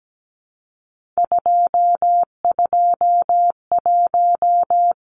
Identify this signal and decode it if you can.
Morse code, heard as a keyed tone, about 17 wpm, 700 Hz. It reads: 221
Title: Numbers - 3-Digit 17wpm